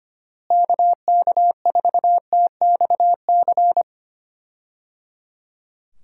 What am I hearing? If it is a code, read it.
Morse code, heard as a keyed tone, about 25 wpm, 700 Hz. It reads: KK4TXC
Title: USA Call Signs - 6 Characters - Repeated 3x - 25wpm